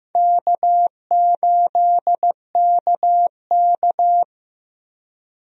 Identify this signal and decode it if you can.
Morse code, heard as a keyed tone, about 15 wpm, 700 Hz. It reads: K8KK
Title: USA Call Signs - 4 Characters - Repeated 2x - 15wpm